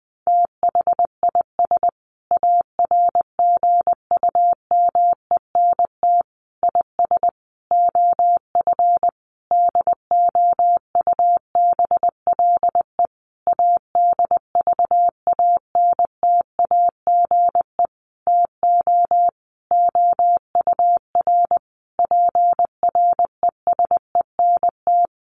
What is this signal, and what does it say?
Morse code, heard as a keyed tone, about 20 wpm, 700 Hz. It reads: THIS ARGUMENT IS OF DOUBLE ADVANTAGE TO OUR PRESENT
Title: Sentences from Top 1400 Words 20wpm